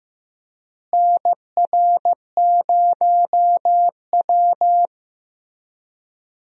Morse code, heard as a keyed tone, about 15 wpm, 700 Hz. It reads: NR0W
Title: USA Call Signs - 4 Characters - Repeated 2x - 15wpm